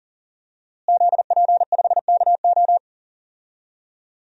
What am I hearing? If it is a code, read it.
Morse code, heard as a keyed tone, about 40 wpm, 700 Hz. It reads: ZP5KO
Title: DXCC-100 Call Signs - 5 Characters - Repeated 3x - 40wpm